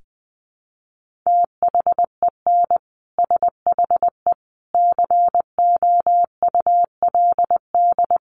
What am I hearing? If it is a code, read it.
Morse code, heard as a keyed tone, about 20 wpm, 700 Hz. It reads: THEN SHE COULD
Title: Sentences from Top 1400 Words 20wpm